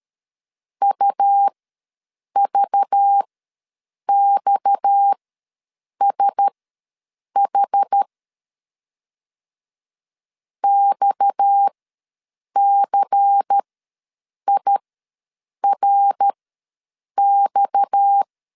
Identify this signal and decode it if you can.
Morse code, heard as a keyed tone, about 13 wpm, spaced as for 7 wpm, 800 Hz. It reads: UVXSH XCIRX
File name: LEZIONI / 6 PKX0 / 6 PKX ws 12wpm 3106